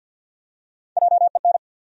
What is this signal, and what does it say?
Morse code, heard as a keyed tone, about 53 wpm, 700 Hz. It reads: JEN
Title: USA Common Names - 50wpm